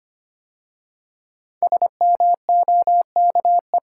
Morse code, heard as a keyed tone, about 25 wpm, 700 Hz. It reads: SMOKE
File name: 5 Letter Words - First 3 Letters Given - 25wpm